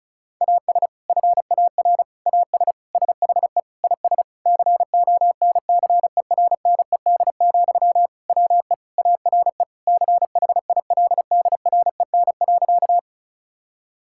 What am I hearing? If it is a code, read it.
Morse code, heard as a keyed tone, about 36 wpm, 700 Hz. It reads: AS FAR AS SHE IS CONCERNED, WE ARE CHILDREN.
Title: Sentences from Top 1000 Words - Review Part 6 - 35wpm